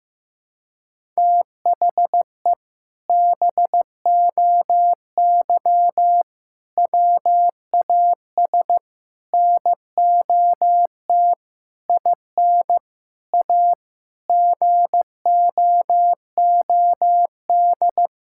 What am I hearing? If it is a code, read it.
Morse code, heard as a keyed tone, about 15 wpm, 700 Hz. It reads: THE BOY WAS NOT IN A GOOD
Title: Sentences from Top 1000 Words - Review Part 6 - 15wpm